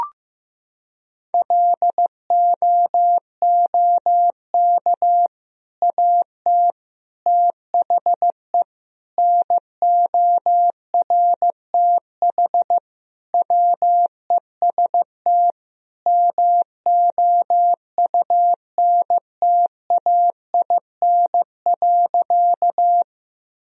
Morse code, heard as a keyed tone, about 15 wpm, 700 Hz. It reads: LOOK AT THE NORTH WEST MOUNTAIN.
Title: Sentences from Top 1000 Words - Review Part 2 - 15wpm